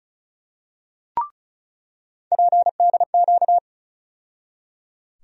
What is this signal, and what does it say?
Morse code, heard as a keyed tone, about 36 wpm, 700 Hz. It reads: PDQ
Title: USA Call Sign Postfixes - 3 Letters 35wpm